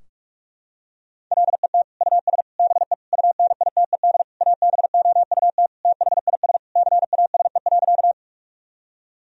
Morse code, heard as a keyed tone, about 46 wpm, 700 Hz. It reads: LET US BE UNITED ABOUT THIS CASE.
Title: Sentences from Top 1000 Words - Review Part 4 - 45wpm